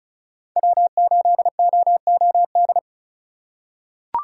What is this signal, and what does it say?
Morse code, heard as a keyed tone, about 36 wpm, 700 Hz. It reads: W8OOD
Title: USA Call Signs - 5 Characters - 4x Character Spacing - 35wpm